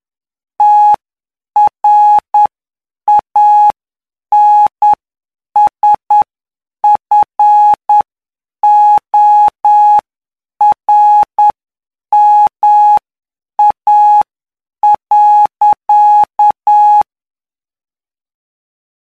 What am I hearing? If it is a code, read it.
Morse code, heard as a keyed tone, about 9 wpm, spaced as for 7 wpm, 840 Hz. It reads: TRANSFORMA.
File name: cw / py2lcd / 17-PY2LCD-Li��o 26